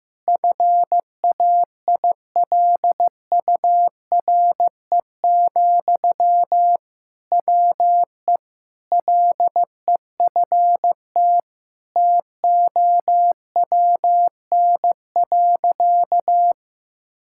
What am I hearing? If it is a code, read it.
Morse code, heard as a keyed tone, about 15 wpm, 700 Hz. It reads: FAILURE, WE LEFT TOWN.
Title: Sentences from Top 1000 Words - Review Part 6 - 15wpm